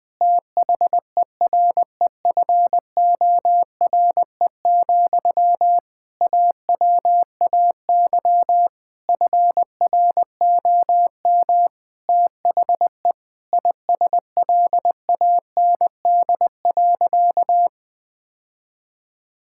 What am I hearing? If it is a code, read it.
Morse code, heard as a keyed tone, about 20 wpm, 700 Hz. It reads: THEREFORE, AWAY FROM THE ISLAND.
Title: Sentences from Top 1000 Words - Review Part 5 - 20wpm